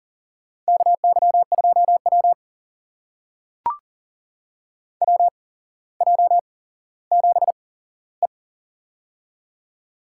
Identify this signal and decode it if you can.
Morse code, heard as a keyed tone, about 40 wpm, spaced as for 17 wpm, 700 Hz. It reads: KY2W WJ7E
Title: USA Call Signs - 4 Characters - 8x Character Spacing - 40wpm